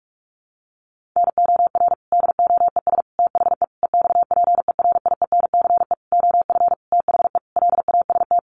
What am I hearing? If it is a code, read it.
Morse code, heard as a keyed tone, about 44 wpm, 700 Hz. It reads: NOR DOES THE EXPERIENCE OF THE LAST
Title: Sentences from Top 1500 Words 45wpm